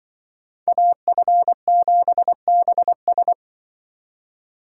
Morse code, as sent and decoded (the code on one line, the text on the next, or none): .- ..-. --... -... ...
AF7BS